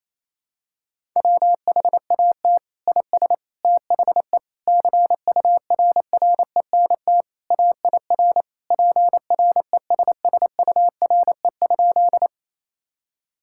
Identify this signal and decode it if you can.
Morse code, heard as a keyed tone, about 28 wpm, 700 Hz. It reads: WHAT IS THE CURRENT AIR PRESSURE?
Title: Sentences from Top 1000 Words - Review Part 6 - 28wpm